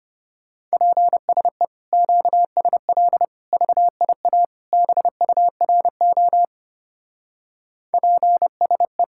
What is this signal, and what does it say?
Morse code, heard as a keyed tone, about 32 wpm, 700 Hz. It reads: PSE QSL VIA BURO PSE